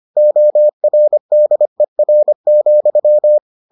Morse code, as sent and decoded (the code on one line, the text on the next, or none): --- .-. -.. .-. --..--
ORDER,